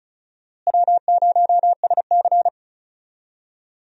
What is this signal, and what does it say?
Morse code, heard as a keyed tone, about 34 wpm, 700 Hz. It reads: W0SC
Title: USA Call Signs - 4 Characters - Repeated 3x - 35wpm